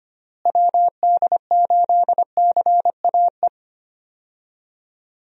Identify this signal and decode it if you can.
Morse code, heard as a keyed tone, about 25 wpm, 700 Hz. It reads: WD8CAE
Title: USA Call Signs - 6 Characters - Repeated 3x - 25wpm